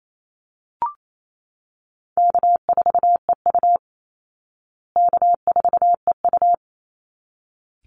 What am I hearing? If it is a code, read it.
Morse code, heard as a keyed tone, about 28 wpm, 700 Hz. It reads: K4EU K4EU
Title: Easy Call Signs Repeated 2x - 28wpm